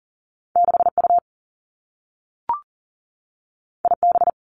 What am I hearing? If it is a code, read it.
Morse code, heard as a keyed tone, about 38 wpm, 700 Hz. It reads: 6 U I B
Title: Two Challenging Letter-Number Combinations 40wpm